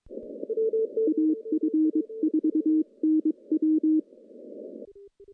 Morse code, heard as a keyed tone, about 23 wpm, 310 Hz. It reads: AF4NW
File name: contests / CW-SS-k5tr- -0400